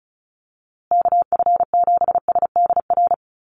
Morse code, heard as a keyed tone, about 34 wpm, 700 Hz. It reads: KF7SDR
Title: USA Call Signs Repeated 2x 35wpm